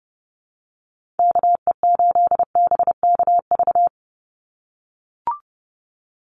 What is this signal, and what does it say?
Morse code, heard as a keyed tone, about 31 wpm, 700 Hz. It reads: KE8BKV
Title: USA Call Signs 30wpm